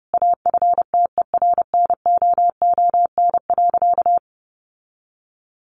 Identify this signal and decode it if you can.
Morse code, heard as a keyed tone, about 30 wpm, 700 Hz. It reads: AFTERNOON.
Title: Sentences from Top 1500 Words 30wpm